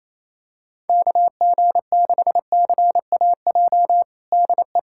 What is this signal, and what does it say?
Morse code, heard as a keyed tone, about 28 wpm, 700 Hz. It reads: KG6CAJ DE
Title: USA Callsign DE Callsign - 6 Characters - 28wpm